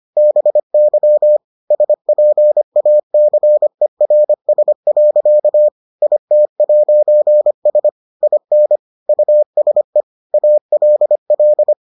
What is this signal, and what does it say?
Morse code, heard as a keyed tone, about 25 wpm, 600 Hz. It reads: BY SPACERS. IT'S IN USE ALL